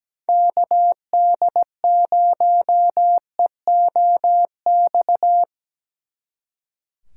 Morse code, heard as a keyed tone, about 17 wpm, 700 Hz. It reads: KD0EOX
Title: Speed Racing - USA Call Signs 17wpm